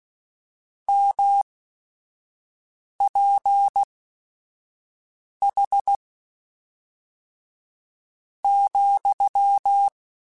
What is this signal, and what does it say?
Morse code, heard as a keyed tone, about 16 wpm, spaced as for 4 wpm, 800 Hz. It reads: MPH,
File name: Spaced Characters / Spaced Characters-mp3s / Weather5 22 sp.@5